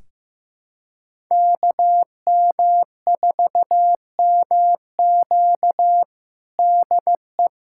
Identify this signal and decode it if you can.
Morse code, heard as a keyed tone, about 15 wpm, 700 Hz. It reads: KM4MQ DE
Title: USA Callsign DE Callsign - 5 Characters - 15wpm